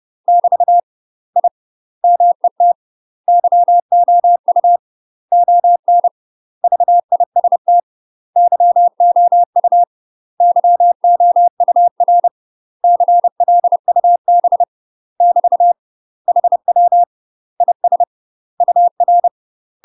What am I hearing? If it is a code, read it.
Morse code, heard as a keyed tone, about 30 wpm, 700 Hz. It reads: <BT> I MET YOU ON VIST YOU YOUR CLUB <BT> HW IS UR
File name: UK MASTER QSOs TRAINING/30WPM